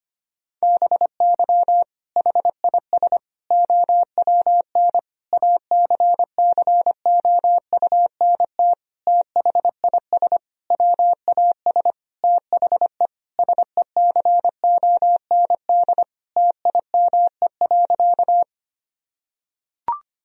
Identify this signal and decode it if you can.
Morse code, heard as a keyed tone, about 25 wpm, 700 Hz. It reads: BY HIS OWN ACCOUNT THIS WAS THE SECOND TIME.
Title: Sentences from Top 1000 Words - Review Part 1 - 25wpm